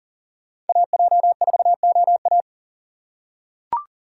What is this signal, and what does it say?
Morse code, heard as a keyed tone, about 40 wpm, 700 Hz. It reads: AJ4OA
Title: USA Call Signs - 5 Characters - 6x Character Spacing - 40wpm